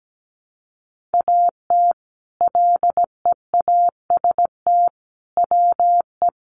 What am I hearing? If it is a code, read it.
Morse code, heard as a keyed tone, about 17 wpm, 700 Hz. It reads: AT LEAST WE
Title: Sentences from Top 1300 Words 17wpm